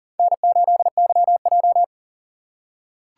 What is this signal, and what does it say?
Morse code, heard as a keyed tone, about 41 wpm, 690 Hz. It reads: N8YJ
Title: USA Call Signs - 4 Characters - 2x Character Spacing - 40wpm